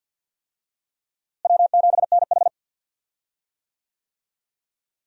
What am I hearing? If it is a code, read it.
Morse code, heard as a keyed tone, about 51 wpm, 700 Hz. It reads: W7NH
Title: USA Call Signs - 4 Characters - Repeated 3x - 50wpm